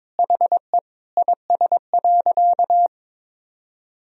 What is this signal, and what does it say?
Morse code, heard as a keyed tone, about 22 wpm, 700 Hz. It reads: HE IS.
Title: Sentences from Top 1000 Words - Review Part 1 - 22wpm